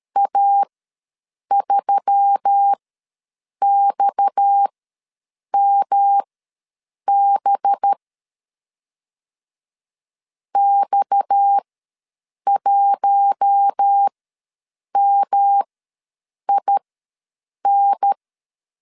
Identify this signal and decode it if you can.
Morse code, heard as a keyed tone, about 13 wpm, spaced as for 7 wpm, 800 Hz. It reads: A3XMB X1MIN
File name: LEZIONI / 6 PKX0 / 6 PKX ws 12wpm 3106